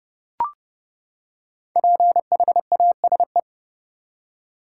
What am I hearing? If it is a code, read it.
Morse code, heard as a keyed tone, about 31 wpm, 700 Hz. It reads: PHASE
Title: Five Letter Words - 30wpm